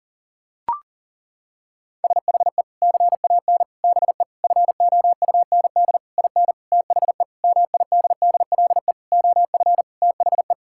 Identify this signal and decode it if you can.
Morse code, heard as a keyed tone, about 37 wpm, 700 Hz. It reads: SHE CAN BE FOUND IN THE MIDDLE OF THE
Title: Sentences from Top 1000 Words - Review Part 6 - 40wpm